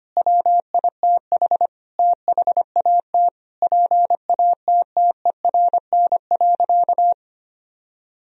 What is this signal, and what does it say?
Morse code, heard as a keyed tone, about 25 wpm, 700 Hz. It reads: WITH THAT PATTERN.
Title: Sentences from Top 1000 Words - Review Part 2 - 25wpm